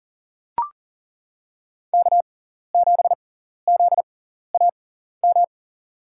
Morse code, heard as a keyed tone, about 40 wpm, spaced as for 14 wpm, 700 Hz. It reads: K7ZAM
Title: USA Call Signs - 5 Characters - 6x Character Spacing - 40wpm